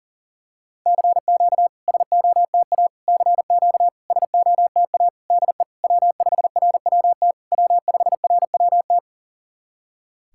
Morse code, heard as a keyed tone, about 40 wpm, 700 Hz. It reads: CQ SOTA CQ SOTA DE W5RWT W5RWT
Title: SOTA QSO Practice 40wpm